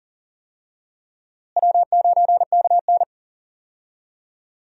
Morse code, heard as a keyed tone, about 40 wpm, 700 Hz. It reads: W9KN
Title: USA Call Signs - 4 Characters - Repeated 2x - 40wpm